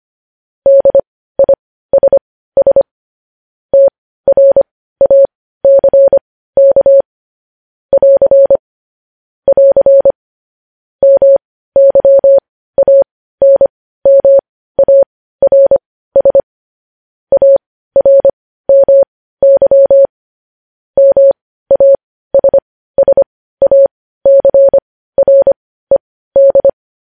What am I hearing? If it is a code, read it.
Morse code, heard as a keyed tone, about 25 wpm, spaced as for 15 wpm, 550 Hz. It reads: DISS TRACK <AR> <AR> MYANMARS ARMY MASSACRED
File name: podcast / News headlines-1x-25x15